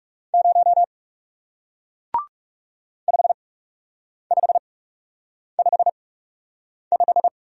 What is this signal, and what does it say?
Morse code, heard as a keyed tone, about 37 wpm, spaced as for 8 wpm, 700 Hz. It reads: 0 5555
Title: Speed-Racing Single Letter-Number 30wpm